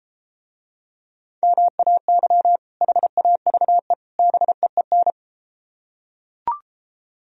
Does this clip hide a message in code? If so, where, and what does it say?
Yes: Morse code from 1.4 to 5.1 s, heard as a keyed tone, about 32 wpm, 700 Hz. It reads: MAY HAVE BEEN